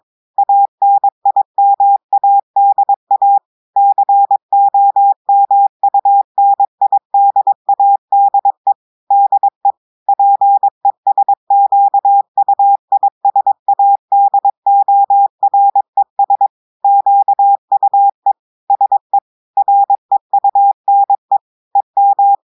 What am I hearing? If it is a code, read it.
Morse code, heard as a keyed tone, about 22 wpm, 800 Hz. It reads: ANIMADA COMUNIDADE DE PESQUISADORES QUE SE REUNE EM
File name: audios / portugais / portugais texte en clair 1 vitesse 22 mots minutes